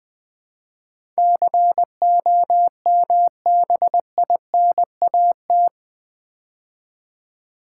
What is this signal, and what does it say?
Morse code, heard as a keyed tone, about 20 wpm, 700 Hz. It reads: COMBINAT
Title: Build A Word - 4 Letters At A Time - 11-Letter Words - 20wpm